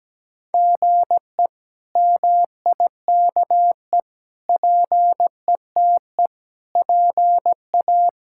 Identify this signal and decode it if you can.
Morse code, heard as a keyed tone, about 17 wpm, 700 Hz. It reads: GE MIKE PETE PA
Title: K1USN SST QSO Practice - 17wpm